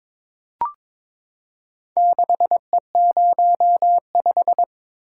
Morse code, heard as a keyed tone, about 23 wpm, 700 Hz. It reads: 6E05
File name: practice / 22wpm / Four Letter-Number Combinations - 22wpm